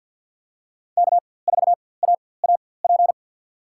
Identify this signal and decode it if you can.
Morse code, heard as a keyed tone, about 49 wpm, spaced as for 23 wpm, 700 Hz. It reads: K4AAP
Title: USA Call Signs - 5 Characters - 4x Character Spacing - 50wpm